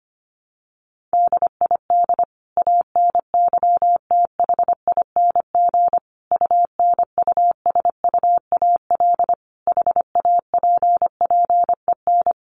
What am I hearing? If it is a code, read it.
Morse code, heard as a keyed tone, about 25 wpm, 700 Hz. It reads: DID ANYTHING UNUSUAL HAPPEN
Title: Sentences from Top 1400 Words 25wpm